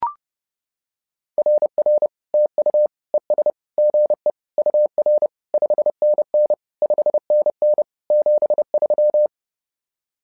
Morse code, heard as a keyed tone, about 30 wpm, 600 Hz. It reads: RR TU ES GE UR 5NN 5NN 73
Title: SOTA QSO Practice 30wpm